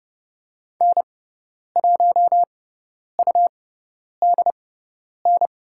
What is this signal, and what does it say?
Morse code, heard as a keyed tone, about 29 wpm, spaced as for 10 wpm, 700 Hz. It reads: N1UDN
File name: Lesson 18g - Letter m - Call Signs 30wpm